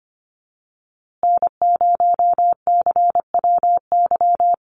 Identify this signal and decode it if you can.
Morse code, heard as a keyed tone, about 25 wpm, 700 Hz. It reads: N0CWY
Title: USA Call Signs Repeated 3x 25wpm